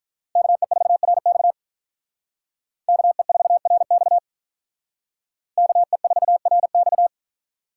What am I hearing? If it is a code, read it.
Morse code, heard as a keyed tone, about 43 wpm, 700 Hz. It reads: KE4RX KE4RX KE4RX
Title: Speed Racing - USA 2x2 Call Signs 35wpm